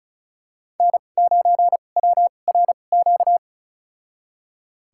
Morse code, heard as a keyed tone, about 34 wpm, spaced as for 25 wpm, 700 Hz. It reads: N9WRQ
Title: USA Call Signs - 5 Characters - 2x Character Spacing - 35wpm